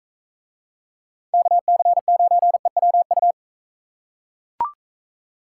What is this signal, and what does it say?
Morse code, heard as a keyed tone, about 42 wpm, 700 Hz. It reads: KC9EWU